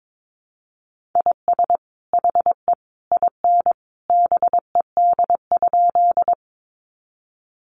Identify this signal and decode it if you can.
Morse code, heard as a keyed tone, about 22 wpm, 700 Hz. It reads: IS HE IN BED?
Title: Sentences from Top 500 Words 22wpm